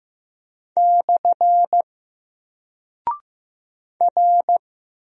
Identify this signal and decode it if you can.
Morse code, heard as a keyed tone, about 15 wpm, 700 Hz. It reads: / R
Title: Single Character (Letter:Number:Punctuation:Prosign) Alt 15wpm